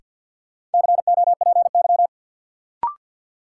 Morse code, heard as a keyed tone, about 53 wpm, 700 Hz. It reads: COPY
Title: Sentences from Top 1400 Words - New Word Drill - 50wpm